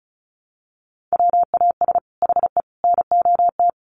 Morse code, heard as a keyed tone, about 34 wpm, 700 Hz. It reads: WAS HE NOT
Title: Sentences from Top 1400 Words 35wpm